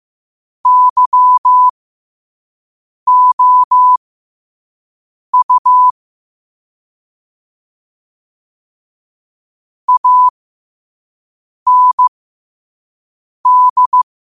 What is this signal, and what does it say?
Morse code, heard as a keyed tone, about 15 wpm, spaced as for 5 wpm, 1000 Hz. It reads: YOU AND
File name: qotdfiles / 5wpm